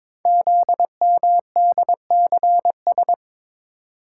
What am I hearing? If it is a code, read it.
Morse code, heard as a keyed tone, about 22 wpm, 700 Hz. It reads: ZMDCS
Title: Code Groups - 5 Letters - 22wpm